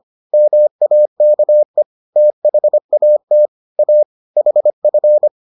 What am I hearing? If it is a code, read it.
Morse code, heard as a keyed tone, about 25 wpm, 600 Hz. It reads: MAKE THAT A HF